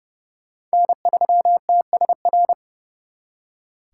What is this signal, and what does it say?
Morse code, heard as a keyed tone, about 35 wpm, 700 Hz. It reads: N3TSR
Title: Speed Racing - USA Call Signs 30wpm